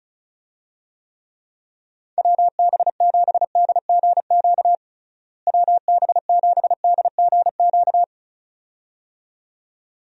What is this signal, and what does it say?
Morse code, heard as a keyed tone, about 36 wpm, 700 Hz. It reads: WB7DGQ WB7DGQ
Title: USA Call Signs - 6 Characters - Repeated 2x - 35wpm